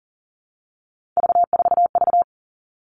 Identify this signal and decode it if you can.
Morse code, heard as a keyed tone, about 40 wpm, 700 Hz. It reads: V4V
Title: All Cut Numbers - 3-Digit 40wpm